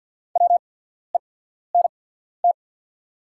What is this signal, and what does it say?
Morse code, heard as a keyed tone, about 50 wpm, spaced as for 14 wpm, 690 Hz. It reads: WENT
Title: 8x Character Spacing - Top 200 Words - 50wpm